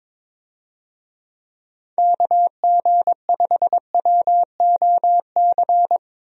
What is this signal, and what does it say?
Morse code, heard as a keyed tone, about 22 wpm, 700 Hz. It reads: KG5WOC
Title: USA Call Signs - 6 Characters - Repeated 3x - 22wpm